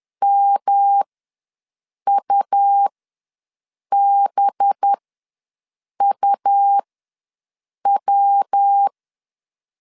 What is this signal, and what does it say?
Morse code, heard as a keyed tone, about 10 wpm, spaced as for 6 wpm, 800 Hz. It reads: MUBUW
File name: LEZIONI / 5 LBVC / 5 LB ws 10wpm 3616